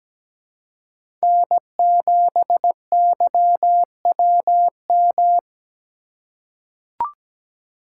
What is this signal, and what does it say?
Morse code, heard as a keyed tone, about 17 wpm, 700 Hz. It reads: N7YWM
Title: USA Call Signs - 3x Character Spacing - 17wpm